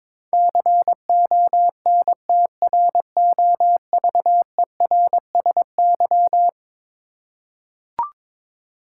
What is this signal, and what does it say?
Morse code, heard as a keyed tone, about 22 wpm, 700 Hz. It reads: CONTROVERSY
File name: practice / 22wpm / Eleven Letter Words - 22wpm